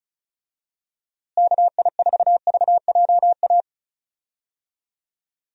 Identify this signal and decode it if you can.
Morse code, heard as a keyed tone, about 36 wpm, 700 Hz. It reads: KI4VJA
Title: USA Call Signs - 6 Characters - Repeated 2x - 35wpm